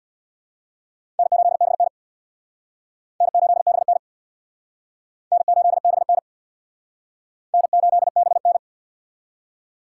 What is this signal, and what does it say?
Morse code, heard as a keyed tone, about 63 wpm, 700 Hz. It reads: N8BN N8BN N8BN N8BN